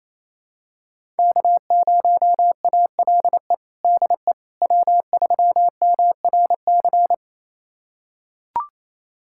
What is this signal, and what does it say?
Morse code, heard as a keyed tone, about 28 wpm, 700 Hz. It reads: K0ALE DE W3MRC
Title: USA Callsign DE Callsign - 5 Characters - 28wpm